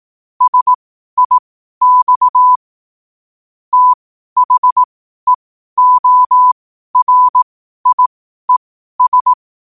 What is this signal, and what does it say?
Morse code, heard as a keyed tone, about 18 wpm, spaced as for 12 wpm, 1000 Hz. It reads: SIX THEORIES
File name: qotdfiles / 12wpm